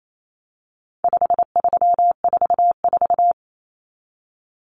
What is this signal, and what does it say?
Morse code, heard as a keyed tone, about 28 wpm, 700 Hz. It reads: 5344
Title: Numbers - 4-Digit 28wpm